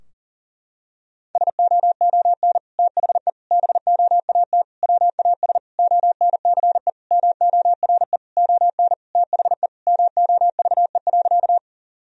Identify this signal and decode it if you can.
Morse code, heard as a keyed tone, about 40 wpm, 700 Hz. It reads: SOON THE BOAT WAS ONCE MORE ON THE MOVE.
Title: Sentences from Top 1000 Words - Review Part 3 - 40wpm